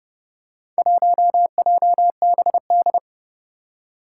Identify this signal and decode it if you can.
Morse code, heard as a keyed tone, about 31 wpm, 700 Hz. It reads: 1JBD
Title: Four Challenging Letter-Number Combinations 30wpm